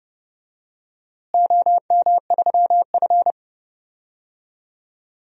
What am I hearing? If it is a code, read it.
Morse code, heard as a keyed tone, about 31 wpm, 700 Hz. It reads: OM3F
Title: DXCC-100 Call Signs - 4 Characters - Repeated 2x - 30wpm